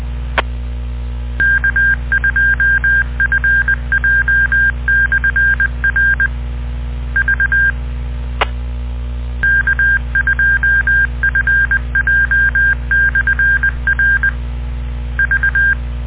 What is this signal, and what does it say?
Morse code, heard as a keyed tone, about 20 wpm, 1600 Hz. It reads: K2FJ/R V K2FJ/R V